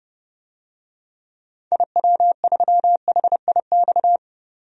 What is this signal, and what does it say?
Morse code, heard as a keyed tone, about 30 wpm, 700 Hz. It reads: IW3HIX
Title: DXCC-100 Call Signs - 6 Characters - 30wpm